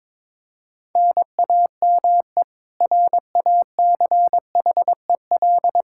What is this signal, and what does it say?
Morse code, heard as a keyed tone, about 22 wpm, 700 Hz. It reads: NAME RACHEL
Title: Practice QSO - Exchange 1 - Beginning - 22wpm